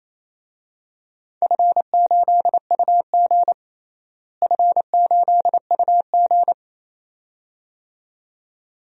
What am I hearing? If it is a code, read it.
Morse code, heard as a keyed tone, about 28 wpm, 700 Hz. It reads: F8UG F8UG
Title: DXCC-100 Call Signs - 4 Characters - Repeated 2x - 28wpm